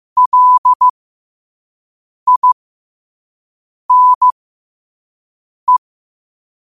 Morse code, heard as a keyed tone, about 15 wpm, spaced as for 5 wpm, 1000 Hz. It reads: LINE
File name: qotdfiles / 5wpm